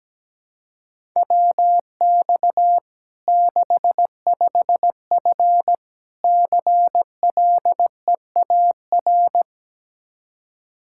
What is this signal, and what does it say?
Morse code, heard as a keyed tone, about 17 wpm, 700 Hz. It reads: WX 65F CLEAR
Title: Practice QSO - Exchange 2 - Info - 17wpm